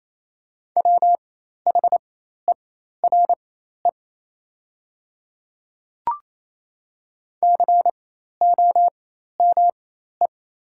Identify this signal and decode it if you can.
Morse code, heard as a keyed tone, about 28 wpm, spaced as for 13 wpm, 700 Hz. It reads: WHERE COME
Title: 4x Character Spacing - Top 300 Words - 28wpm